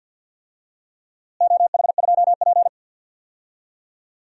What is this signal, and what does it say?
Morse code, heard as a keyed tone, about 51 wpm, 700 Hz. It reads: OH2P
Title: DXCC-100 Call Signs - 4 Characters - Repeated 3x - 50wpm